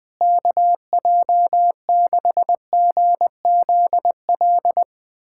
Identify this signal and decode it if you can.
Morse code, heard as a keyed tone, about 21 wpm, 700 Hz. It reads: KJ6GZL
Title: USA Callsign DE Callsign 20wpm